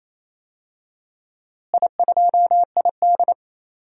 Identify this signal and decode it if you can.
Morse code, heard as a keyed tone, about 28 wpm, 700 Hz. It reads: I2ID
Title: DXCC-100 Call Signs - 4 Characters - Repeated 2x - 28wpm